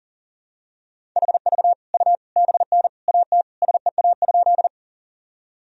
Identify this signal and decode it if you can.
Morse code, heard as a keyed tone, about 37 wpm, 700 Hz. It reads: HV U BN AT SEA?
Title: Sentences from Top 300 Words and Abbreviations 40wpm